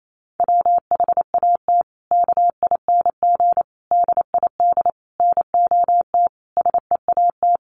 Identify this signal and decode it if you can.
Morse code, heard as a keyed tone, about 28 wpm, 700 Hz. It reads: WHAT KING DID NOT SEAT
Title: Sentences from Top 1500 Words 28wpm